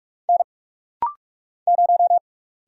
Morse code, heard as a keyed tone, about 44 wpm, 700 Hz. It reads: N 0
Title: Single Letter-Number - Rapid-Fire 45wpm